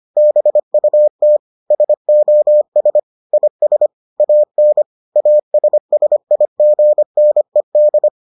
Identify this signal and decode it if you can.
Morse code, heard as a keyed tone, about 25 wpm, 600 Hz. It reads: BUT SOS IS AN ASSIGNED